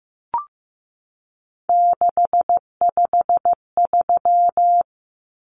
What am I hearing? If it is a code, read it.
Morse code, heard as a keyed tone, about 15 wpm, 700 Hz. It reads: 653
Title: Numbers - 3-Digit 15wpm